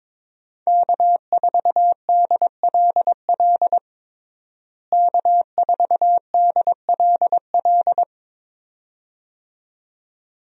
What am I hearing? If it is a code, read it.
Morse code, heard as a keyed tone, about 22 wpm, 700 Hz. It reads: K4DLL K4DLL
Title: USA Call Signs - 5 Characters - Repeated 2x - 22wpm